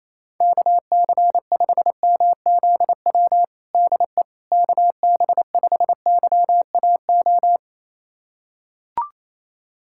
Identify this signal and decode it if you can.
Morse code, heard as a keyed tone, about 28 wpm, 700 Hz. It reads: KC5MZW DE KB5YAO
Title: USA Callsign DE Callsign - 6 Characters - 28wpm